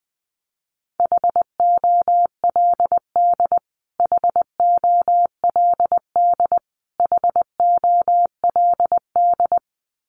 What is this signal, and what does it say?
Morse code, heard as a keyed tone, about 20 wpm, 700 Hz. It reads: HOLD HOLD HOLD
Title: Top 500 Words Repeated 20wpm